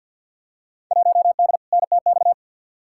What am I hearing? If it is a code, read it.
Morse code, heard as a keyed tone, about 53 wpm, 700 Hz. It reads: 1D NTX
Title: Field Day Exchanges - 50wpm